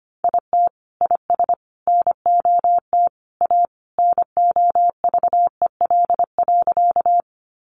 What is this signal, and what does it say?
Morse code, heard as a keyed tone, about 25 wpm, 700 Hz. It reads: IT IS NOT A NOVEL.
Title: Sentences from Top 1300 Words 25wpm